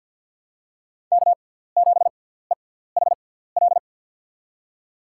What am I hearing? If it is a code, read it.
Morse code, heard as a keyed tone, about 52 wpm, spaced as for 17 wpm, 700 Hz. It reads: K7EHL
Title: USA Call Signs - 5 Characters - 6x Character Spacing - 50wpm